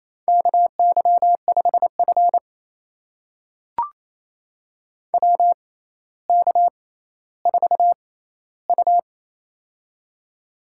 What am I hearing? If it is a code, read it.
Morse code, heard as a keyed tone, about 28 wpm, spaced as for 14 wpm, 700 Hz. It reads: KY5F WK4U
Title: USA Call Signs - 4 Characters - 6x Character Spacing - 28wpm